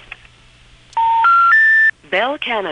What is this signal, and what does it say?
Special information tone, with speech.